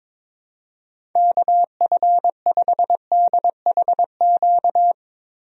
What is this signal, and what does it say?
Morse code, heard as a keyed tone, about 22 wpm, 700 Hz. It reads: KF5DHQ
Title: USA Call Signs - 6 Characters - Repeated 3x - 22wpm